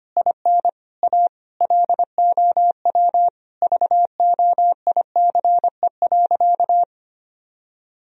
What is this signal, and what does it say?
Morse code, heard as a keyed tone, about 25 wpm, 700 Hz. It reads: IN A LOW VOICE.
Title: Sentences from Top 1000 Words - Review Part 1 - 25wpm